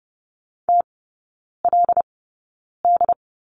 Morse code, heard as a keyed tone, about 30 wpm, spaced as for 9 wpm, 700 Hz. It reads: TLD
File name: Lesson 18e - Letter m - Three Characters 30wpm